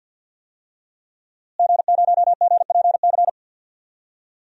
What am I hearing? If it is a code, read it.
Morse code, heard as a keyed tone, about 52 wpm, 700 Hz. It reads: G0GPC
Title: DXCC-100 Call Signs - 5 Characters - Repeated 3x - 50wpm